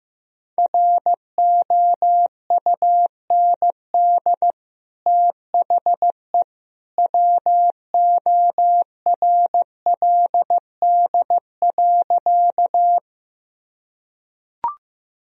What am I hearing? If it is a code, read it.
Morse code, heard as a keyed tone, about 15 wpm, 700 Hz. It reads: ROUND THE WORLD.
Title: Sentences from Top 1000 Words - Review Part 1 - 15wpm